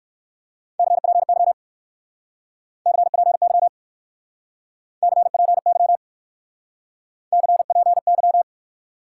Morse code, heard as a keyed tone, about 58 wpm, 700 Hz. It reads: CPY CPY CPY CPY